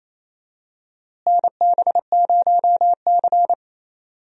Morse code, heard as a keyed tone, about 27 wpm, 700 Hz. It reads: NB0C
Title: USA Call Signs - 4 Characters - Repeated 3x - 28wpm